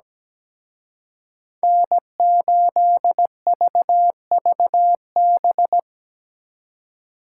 Morse code, heard as a keyed tone, about 20 wpm, 700 Hz. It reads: N8VVB
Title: Speed Racing - USA Call Signs 17wpm